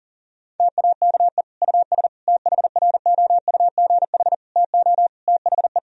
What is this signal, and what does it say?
Morse code, heard as a keyed tone, about 40 wpm, 700 Hz. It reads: TAKE US THROUGH TO THE
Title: Sentences from Top 1000 Words - Review Part 1 - 40wpm